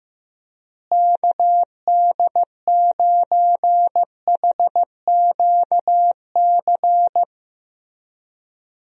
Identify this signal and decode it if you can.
Morse code, heard as a keyed tone, about 15 wpm, 700 Hz. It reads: KD9HQC
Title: USA Call Signs - 6 Characters - Repeated 3x - 15wpm